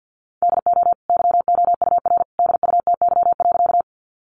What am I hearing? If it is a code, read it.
Morse code, heard as a keyed tone, about 50 wpm, 700 Hz. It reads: DO YOUR DUTY.
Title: Sentences from Top 1500 Words 50wpm